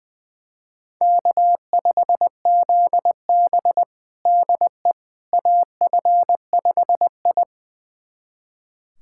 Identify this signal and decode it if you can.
Morse code, heard as a keyed tone, about 20 wpm, 700 Hz. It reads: K5ZB DE AF5I
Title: USA Callsign DE Callsign - 4 Characters - 20wpm